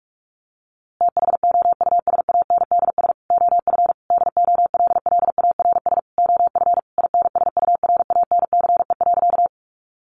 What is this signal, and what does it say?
Morse code, heard as a keyed tone, about 44 wpm, 700 Hz. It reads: THOUSANDS OF DOLLARS OF INSURANCE.
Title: A Sound Of Thunder 45wpm